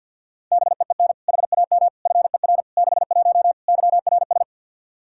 Morse code, heard as a keyed tone, about 49 wpm, 700 Hz. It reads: BEEN HAM FER 61 YRS
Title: Practice QSO - Exchange 3 - More Info - 50wpm